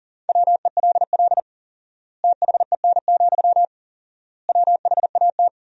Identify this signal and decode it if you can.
Morse code, heard as a keyed tone, about 37 wpm, 700 Hz. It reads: WELL THEN, WHAT
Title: 4x Word-Spacing - Sentences from Top 200 Words - 40wpm